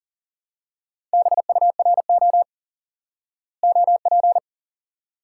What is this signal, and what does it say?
Morse code, heard as a keyed tone, about 40 wpm, 700 Hz. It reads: BURO OP
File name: IQE Course 3 - Lesson 89 - YAGI - Two Elements - 40wpm